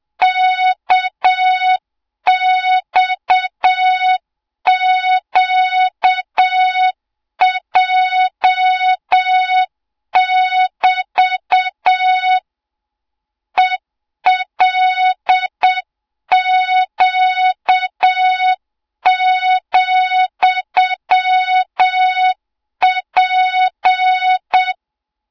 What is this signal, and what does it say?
Morse code, heard as a keyed tone, about 7 wpm, 3700 Hz, with noise. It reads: KXQJ<BT> ELQ,P
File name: Morze-HA2MN / hang / 20 GYAKORLAT